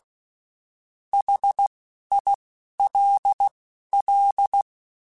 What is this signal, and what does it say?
Morse code, heard as a keyed tone, about 16 wpm, spaced as for 11 wpm, 800 Hz. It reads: HILL
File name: (spaced) / QSOs Spaced-mp3s / WA7QOX 2 @15 sp